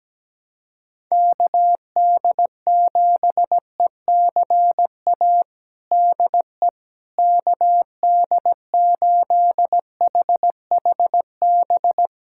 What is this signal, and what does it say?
Morse code, heard as a keyed tone, about 17 wpm, 700 Hz. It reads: KD7ECA DE KD8HHB
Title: USA Callsign DE Callsign - 6 Characters - 17wpm